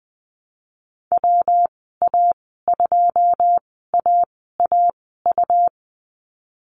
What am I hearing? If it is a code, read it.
Morse code, heard as a keyed tone, about 19 wpm, spaced as for 14 wpm, 700 Hz. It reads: WA2AAU
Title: 147 Lesson 24g - Letter y - Call Signs 20wpm